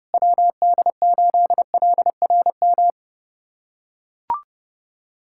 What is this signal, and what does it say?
Morse code, heard as a keyed tone, about 30 wpm, 700 Hz. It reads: WD8LRM
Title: USA Callsign DE Callsign - 6 Characters - 30wpm